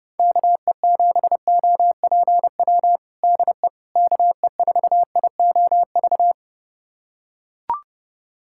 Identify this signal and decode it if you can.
Morse code, heard as a keyed tone, about 30 wpm, 690 Hz. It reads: KE7OPW DE KE4IOV
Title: USA Callsign DE Callsign - 6 Characters - 30wpm